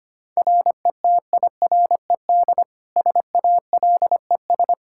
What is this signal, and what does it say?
Morse code, heard as a keyed tone, about 25 wpm, 700 Hz. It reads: RETIRED SALES
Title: Practice QSO - Exchange 3 - More Info - 25wpm